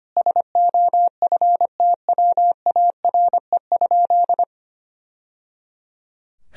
Morse code, heard as a keyed tone, about 25 wpm, 700 Hz. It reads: SOFTWARE?
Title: Sentences from Top 1000 Words - Review Part 3 - 25wpm